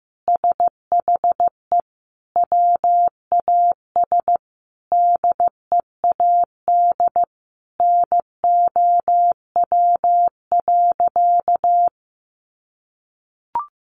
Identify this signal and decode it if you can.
Morse code, heard as a keyed tone, about 14 wpm, 700 Hz. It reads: SHE WAS DEAD NOW.
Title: Sentences from Top 1500 Words 15wpm